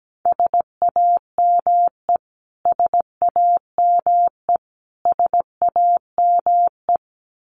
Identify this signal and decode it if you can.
Morse code, heard as a keyed tone, about 17 wpm, 700 Hz. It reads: SAME SAME SAME
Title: Top 500 Words Repeated 17wpm